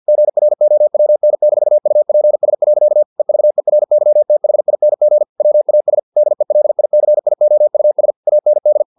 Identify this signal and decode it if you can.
Morse code, heard as a keyed tone, about 50 wpm, 600 Hz. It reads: GROWN-UPS. EVERYTHING WAS DELICIOUS AND
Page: Never 49wpm